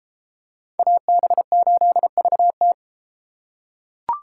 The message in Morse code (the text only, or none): AB8VT